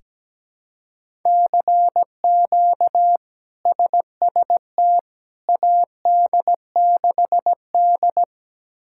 Morse code, heard as a keyed tone, about 17 wpm, 700 Hz. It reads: CQ SST AD6D
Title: K1USN SST QSO Practice - 17wpm